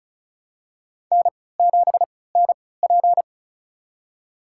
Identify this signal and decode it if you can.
Morse code, heard as a keyed tone, about 36 wpm, spaced as for 20 wpm, 700 Hz. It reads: N7NP
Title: USA Call Signs - 4 Characters - 3x Character Spacing - 35wpm